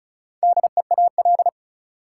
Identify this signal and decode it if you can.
Morse code, heard as a keyed tone, about 34 wpm, 700 Hz. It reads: DEAL
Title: Four Letter Words - 35wpm